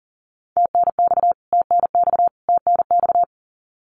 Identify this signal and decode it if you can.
Morse code, heard as a keyed tone, about 40 wpm, 700 Hz. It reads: TNX TNX TNX
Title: Abbreviations Repeated 3x - 40wpm